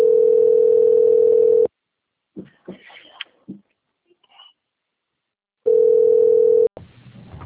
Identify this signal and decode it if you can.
Ringback tone (North America), 440 + 480 Hz.